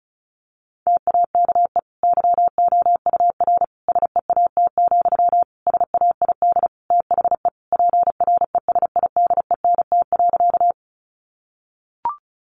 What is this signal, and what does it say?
Morse code, heard as a keyed tone, about 34 wpm, 700 Hz. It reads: TAKE YOUR SEAT, SAID THE PRESIDENT.
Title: Sentences from Top 1500 Words 35wpm